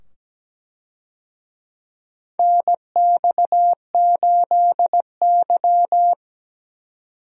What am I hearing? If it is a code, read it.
Morse code, heard as a keyed tone, about 17 wpm, 700 Hz. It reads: NX8Y
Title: USA Call Signs - 4 Characters - 17wpm